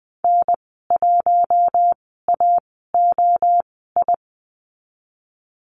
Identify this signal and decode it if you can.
Morse code, heard as a keyed tone, about 20 wpm, spaced as for 15 wpm, 700 Hz. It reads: N1AOI
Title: 048 Lesson 9g - Number 4 - Call Signs 20wpm